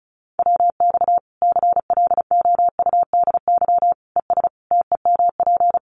Morse code, heard as a keyed tone, about 34 wpm, 700 Hz. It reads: WX CLOUDY ES TEMP
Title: Common QSO Phrases 35wpm